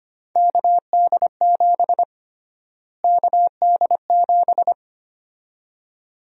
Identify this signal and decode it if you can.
Morse code, heard as a keyed tone, about 25 wpm, 700 Hz. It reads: KD7 KD7
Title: USA Call Sign Prefixes - Repeated 2x - 25wpm